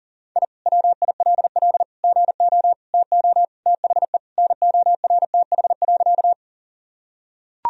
Morse code, heard as a keyed tone, about 37 wpm, 700 Hz. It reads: I WILL GO TO THE NORTH.
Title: Sentences from Top 1000 Words - Review Part 1 - 40wpm